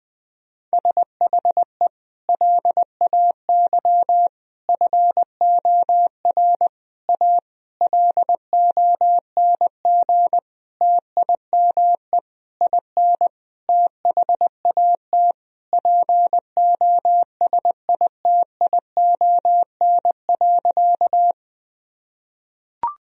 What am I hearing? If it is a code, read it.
Morse code, heard as a keyed tone, about 20 wpm, 700 Hz. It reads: SHE LAY FOR A LONG TIME IN THAT POSITION.
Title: Sentences from Top 1000 Words - Review Part 2 - 20wpm